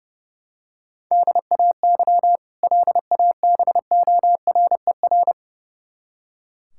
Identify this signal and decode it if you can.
Morse code, heard as a keyed tone, about 30 wpm, 700 Hz. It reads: DAY LABORER
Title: K1USN SST QSO Practice 30wpm